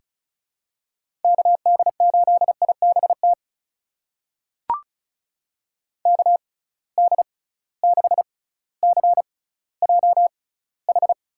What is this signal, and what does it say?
Morse code, heard as a keyed tone, about 34 wpm, spaced as for 18 wpm, 700 Hz. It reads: KD8IBT KD6CJH
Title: USA Call Signs - 6 Characters - 6x Character Spacing - 35wpm